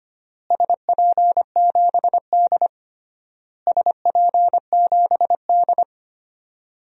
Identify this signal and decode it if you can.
Morse code, heard as a keyed tone, about 25 wpm, 700 Hz. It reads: SP7D SP7D
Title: DXCC-100 Call Signs - 4 Characters - Repeated 3x - 25wpm